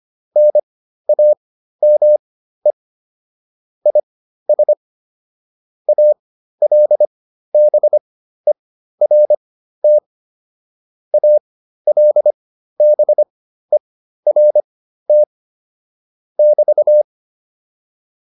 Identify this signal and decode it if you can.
Morse code, heard as a keyed tone, about 25 wpm, spaced as for 13 wpm, 600 Hz. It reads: NAME IS ALBERT ALBERT <BT>
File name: QSO109 13